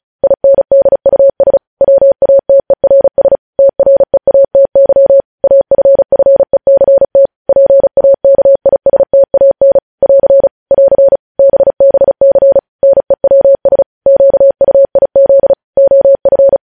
Morse code, heard as a keyed tone, about 36 wpm, 550 Hz. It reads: INDUS WATERS TREATY AFFECT PAKISTAN <AR> <AR> BBC NEWS QUIZ OF
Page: News headlines in Morse code at 35 wpm